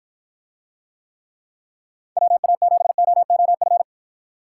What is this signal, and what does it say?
Morse code, heard as a keyed tone, about 53 wpm, 700 Hz. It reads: WA7OOF